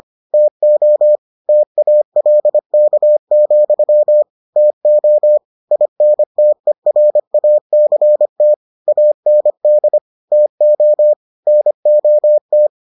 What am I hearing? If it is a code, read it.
Morse code, heard as a keyed tone, about 25 wpm, 600 Hz. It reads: TO TALK, TO INTERACT AND TO NOT